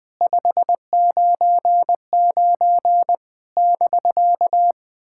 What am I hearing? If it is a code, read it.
Morse code, heard as a keyed tone, about 20 wpm, 700 Hz. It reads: 599 <BK>
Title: POTA QSO Practice - 20wpm